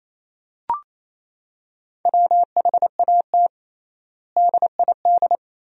Morse code, heard as a keyed tone, about 28 wpm, 700 Hz. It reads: WHAT DID
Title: 3x Word-Spacing - Sentences from Top 100 Words - 28wpm